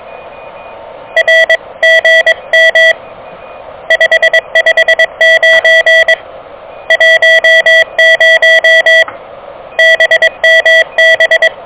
Morse code, heard as a keyed tone, about 22 wpm, 1950 Hz, with noise. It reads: RGM 559 10 BMB
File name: OK QRP 2023 / OK QRP 2023